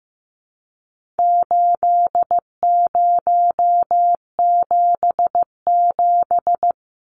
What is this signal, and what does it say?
Morse code, heard as a keyed tone, about 15 wpm, 700 Hz. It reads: 8077
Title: Numbers - 4-Digit 15wpm